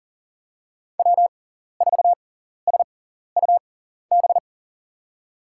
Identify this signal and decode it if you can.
Morse code, heard as a keyed tone, about 40 wpm, spaced as for 14 wpm, 700 Hz. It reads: W4SUB
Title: USA Call Signs - 5 Characters - 6x Character Spacing - 40wpm